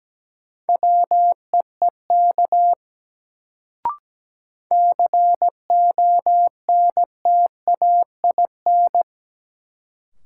Morse code, heard as a keyed tone, about 17 wpm, 700 Hz. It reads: WEEK CONTAIN
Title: Sentences from Top 500 Words - New Word Drill - 17wpm